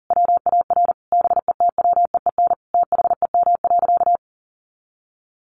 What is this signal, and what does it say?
Morse code, heard as a keyed tone, about 40 wpm, 700 Hz. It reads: WAR BETWEEN THEM.
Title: Sentences from Top 400 Words 40wpm